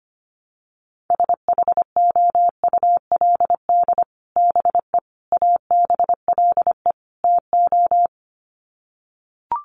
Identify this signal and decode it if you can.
Morse code, heard as a keyed tone, about 25 wpm, 700 Hz. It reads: SHOULD BE ABLE TO
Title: Sets of 4 Words 25wpm